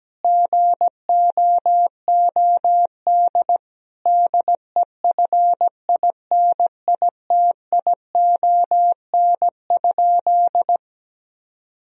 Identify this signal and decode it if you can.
Morse code, heard as a keyed tone, about 17 wpm, 700 Hz. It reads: GOOD DEFINITION?
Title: Sentences from Top 1000 Words - Review Part 4 - 17wpm